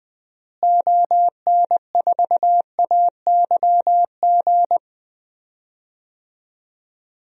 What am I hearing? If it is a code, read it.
Morse code, heard as a keyed tone, about 20 wpm, 700 Hz. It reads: ON4AYG
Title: DXCC-100 Call Signs - 6 Characters - Repeated 2x - 20wpm